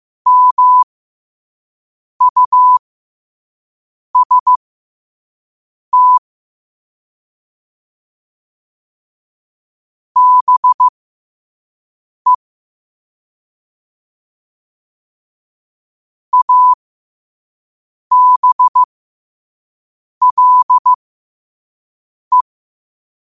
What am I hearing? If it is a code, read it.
Morse code, heard as a keyed tone, about 15 wpm, spaced as for 5 wpm, 1000 Hz. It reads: MUST BE ABLE
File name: qotdfiles / 5wpm